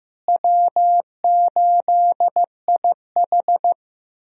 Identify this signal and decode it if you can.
Morse code, heard as a keyed tone, about 16 wpm, 700 Hz. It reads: W8IH
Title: USA Call Signs - 4 Characters - Repeated 2x - 15wpm